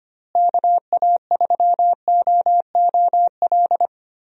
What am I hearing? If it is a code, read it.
Morse code, heard as a keyed tone, about 25 wpm, 700 Hz. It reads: KA3OOL
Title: USA Call Signs - 6 Characters - Repeated 3x - 25wpm